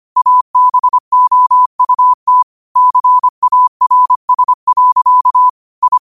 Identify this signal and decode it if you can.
Morse code, heard as a keyed tone, about 25 wpm, 1000 Hz. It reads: ABOUT CARS. I